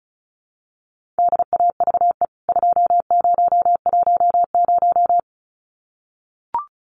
Morse code, heard as a keyed tone, about 34 wpm, 700 Hz. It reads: DAVE 2010
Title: CWT Exchange 35wpm